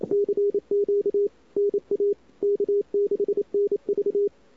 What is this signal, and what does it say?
Morse code, heard as a keyed tone, about 30 wpm, 400 Hz. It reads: CQ NA K6NV
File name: CWSprint / Feb2007 / 2007-feb-cw-sprint-k5tr-op-n5rz -00:00